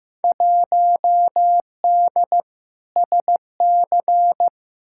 Morse code, heard as a keyed tone, about 15 wpm, 700 Hz. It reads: 1D SC
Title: Field Day Exchanges - 15wpm